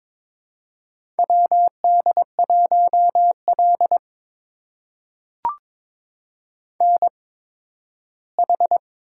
Morse code, heard as a keyed tone, about 22 wpm, spaced as for 13 wpm, 700 Hz. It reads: WD1L NH
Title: USA Call Signs - 4 Characters - 8x Character Spacing - 22wpm